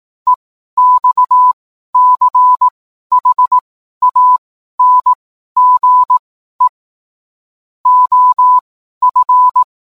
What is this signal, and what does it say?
Morse code, heard as a keyed tone, about 18 wpm, spaced as for 12 wpm, 1000 Hz. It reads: EXCHANGE OF